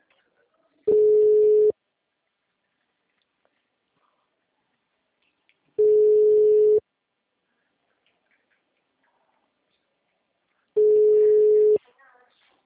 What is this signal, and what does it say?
Ringback tone (Europe, 425 Hz), with speech.